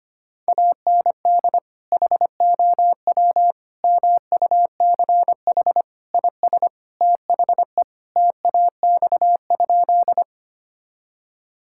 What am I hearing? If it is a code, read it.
Morse code, heard as a keyed tone, about 25 wpm, 700 Hz. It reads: AND HOW MUCH IS THE TAX?
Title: Sentences from Top 1000 Words - Review Part 5 - 25wpm